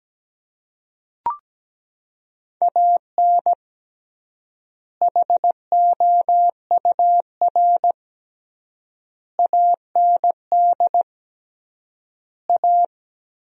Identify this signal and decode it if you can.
Morse code, heard as a keyed tone, about 17 wpm, 700 Hz. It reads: AN HOUR AND A
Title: 3x Word-Spacing - Sentences from Top 400 Words - 17wpm